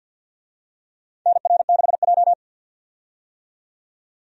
Morse code, heard as a keyed tone, about 51 wpm, 700 Hz. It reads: NR6J
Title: USA Call Signs - 4 Characters - 50wpm